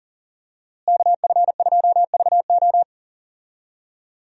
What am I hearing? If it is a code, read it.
Morse code, heard as a keyed tone, about 40 wpm, 700 Hz. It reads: KF2VO
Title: USA Call Signs - 5 Characters - Repeated 3x - 40wpm